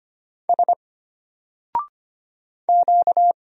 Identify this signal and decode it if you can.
Morse code, heard as a keyed tone, about 25 wpm, 700 Hz. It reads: S Q
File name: Single-Letter-Number-25wpm